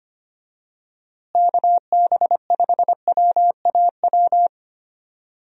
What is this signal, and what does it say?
Morse code, heard as a keyed tone, about 25 wpm, 700 Hz. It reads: KB5WAW
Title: USA Call Signs - 6 Characters - Repeated 3x - 25wpm